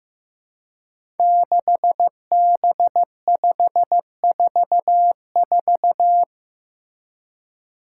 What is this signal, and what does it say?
Morse code, heard as a keyed tone, about 15 wpm, 700 Hz. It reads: 6B544
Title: Five Challenging Letter-Number Combinations 15wpm